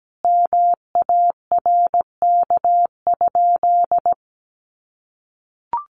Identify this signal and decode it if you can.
Morse code, heard as a keyed tone, about 17 wpm, 700 Hz. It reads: MARK?
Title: Sentences from Top 300 Words 17wpm